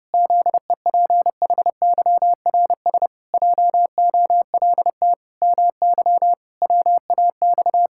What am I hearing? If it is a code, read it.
Morse code, heard as a keyed tone, about 32 wpm, 700 Hz. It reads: ZEPHYRS JOLT MY WAX
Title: Pangram Challenge v2 - 30wpm